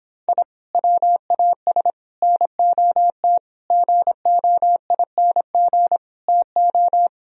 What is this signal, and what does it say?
Morse code, heard as a keyed tone, about 26 wpm, 700 Hz. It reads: I WAS NOT GOING TO